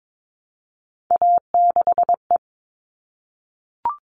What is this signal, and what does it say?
Morse code, heard as a keyed tone, about 22 wpm, 700 Hz. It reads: A6E
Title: All Cut Numbers - 3-Digit 22wpm